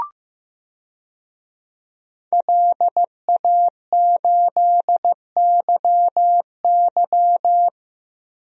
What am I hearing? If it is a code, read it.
Morse code, heard as a keyed tone, about 15 wpm, 700 Hz. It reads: LA8YY
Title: DXCC-100 Call Signs - 5 Characters - Repeated 3x - 15wpm